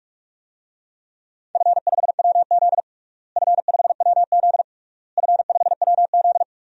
Morse code, heard as a keyed tone, about 46 wpm, 700 Hz. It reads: F5WZ F5WZ F5WZ
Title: DXCC-100 Call Signs - 4 Characters - Repeated 3x - 45wpm